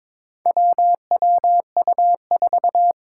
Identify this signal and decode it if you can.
Morse code, heard as a keyed tone, about 22 wpm, 700 Hz. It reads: WWU4
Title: Four Challenging Letter-Number Combinations 22wpm